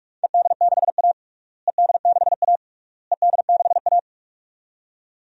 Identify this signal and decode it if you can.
Morse code, heard as a keyed tone, about 46 wpm, 700 Hz. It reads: ED6A ED6A ED6A
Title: DXCC-100 Call Signs - 4 Characters - Repeated 3x - 45wpm